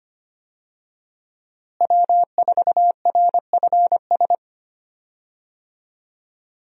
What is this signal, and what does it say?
Morse code, heard as a keyed tone, about 25 wpm, 700 Hz. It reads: W4RFS
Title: USA Call Signs - 5 Characters - Repeated 2x - 25wpm